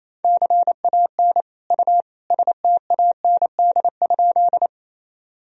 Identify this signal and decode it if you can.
Morse code, heard as a keyed tone, about 28 wpm, 700 Hz. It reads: CAN U STAND?
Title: Sentences from Top 300 Words and Abbreviations 28wpm